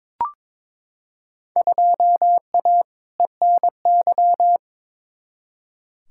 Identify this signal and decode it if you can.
Morse code, heard as a keyed tone, about 22 wpm, 700 Hz. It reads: 2A ENY
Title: Field Day Exchanges Encore - 22wpm